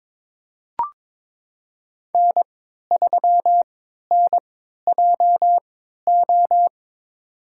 Morse code, heard as a keyed tone, about 22 wpm, spaced as for 12 wpm, 700 Hz. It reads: N3NJO
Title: USA Call Signs - 5 Characters - 3x Character Spacing - 22wpm